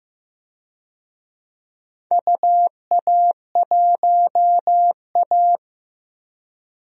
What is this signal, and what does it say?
Morse code, heard as a keyed tone, about 16 wpm, 700 Hz. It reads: UA1A
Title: DXCC-100 Call Signs - 4 Characters - Repeated 3x - 15wpm